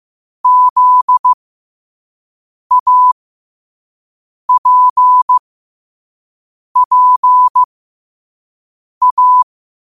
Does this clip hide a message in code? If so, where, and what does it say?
Yes: Morse code from 0.4 to 9.4 s, heard as a keyed tone, about 15 wpm, spaced as for 5 wpm, 1000 Hz. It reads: ZAPPA